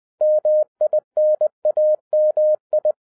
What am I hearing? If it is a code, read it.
Morse code, heard as a keyed tone, about 20 wpm, 610 Hz. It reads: MINAMI